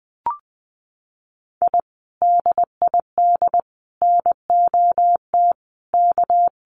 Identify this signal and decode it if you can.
Morse code, heard as a keyed tone, about 20 wpm, 700 Hz. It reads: I DID NOT K
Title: Sets of 5 Words Encore 20wpm